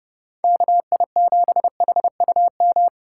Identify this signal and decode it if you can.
Morse code, heard as a keyed tone, about 31 wpm, 700 Hz. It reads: KI7HUM
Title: Speed Racing - USA 2x3 Call Signs 25wpm